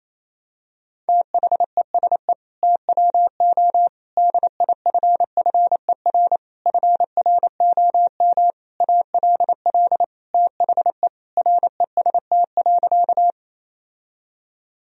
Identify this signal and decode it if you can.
Morse code, heard as a keyed tone, about 28 wpm, 700 Hz. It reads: THESE TWO DIFFER FROM ALL THE REST.
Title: Sentences from Top 1000 Words - Review Part 1 - 28wpm